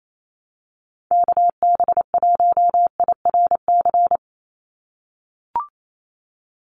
Morse code, heard as a keyed tone, about 28 wpm, 700 Hz. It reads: KB1IRC
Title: USA Call Signs 28wpm